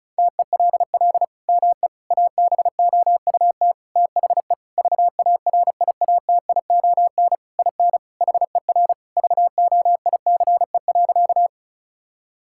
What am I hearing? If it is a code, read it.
Morse code, heard as a keyed tone, about 34 wpm, 700 Hz. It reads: TELL ME ABOUT THE VARIATION IN HER VOICE.
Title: Sentences from Top 1600 Words 35wpm